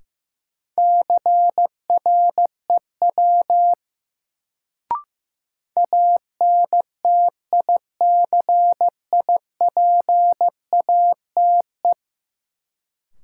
Morse code, heard as a keyed tone, about 15 wpm, 700 Hz. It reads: CREW ANTICIPATE
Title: Sentences from Top 1500 Words - New Word Drill - 15wpm